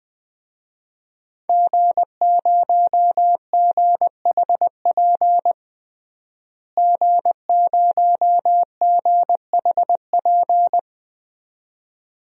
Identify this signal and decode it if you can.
Morse code, heard as a keyed tone, about 20 wpm, 700 Hz. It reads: G0GHP G0GHP
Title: DXCC-100 Call Signs - 5 Characters - Repeated 2x - 20wpm